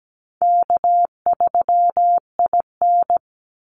Morse code, heard as a keyed tone, about 17 wpm, 700 Hz. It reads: K3IN
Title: Easy Call Signs - 17wpm